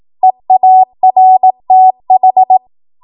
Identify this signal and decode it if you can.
Morse code, heard as a keyed tone, about 18 wpm, 750 Hz. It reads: EARTH
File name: Morse / 18WPM